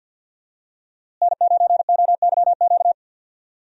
Morse code, heard as a keyed tone, about 52 wpm, 700 Hz. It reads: N9OYQ
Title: USA Call Signs - 5 Characters - Repeated 3x - 50wpm